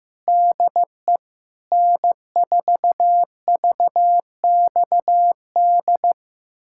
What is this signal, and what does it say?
Morse code, heard as a keyed tone, about 16 wpm, 700 Hz. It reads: DE N4VXD
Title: USA Callsign DE Callsign - 5 Characters - 15wpm